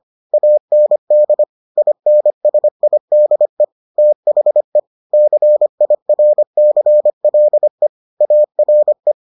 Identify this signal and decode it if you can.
Morse code, heard as a keyed tone, about 25 wpm, 600 Hz. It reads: AND INSIDE THE CIRCLE ARE